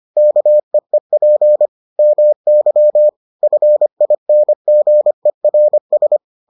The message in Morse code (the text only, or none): KEEP MY FINGERS